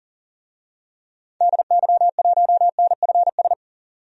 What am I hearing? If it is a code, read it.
Morse code, heard as a keyed tone, about 40 wpm, 700 Hz. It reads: DY1NFS
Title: DXCC-100 Call Signs - 6 Characters - 40wpm